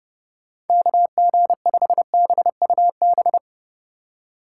Morse code, heard as a keyed tone, about 32 wpm, 700 Hz. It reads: KG5BUB